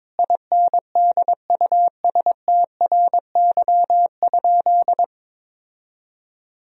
Morse code, heard as a keyed tone, about 22 wpm, 700 Hz. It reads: INDUSTRY?
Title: Sentences from Top 1000 Words - Review Part 4 - 22wpm